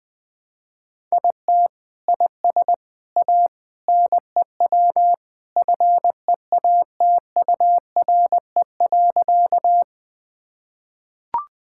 Morse code, heard as a keyed tone, about 20 wpm, 700 Hz. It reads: IT IS A NEW FEATURE.
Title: Sentences from Top 1000 Words - Review Part 5 - 20wpm